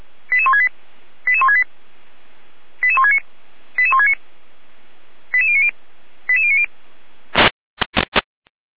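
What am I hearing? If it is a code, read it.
Five-tone selective call (ZVEI), heard as quick runs of five tones, tones of about 70 ms. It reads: 88168 88168 88169 88169 88090 88090
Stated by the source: Five-tone ZVEI selective call received on an SDR.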